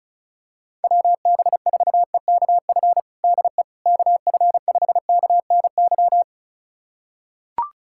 Morse code, heard as a keyed tone, about 34 wpm, 700 Hz. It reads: WB4EKF DE KF5KNY
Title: USA Callsign DE Callsign - 6 Characters - 35wpm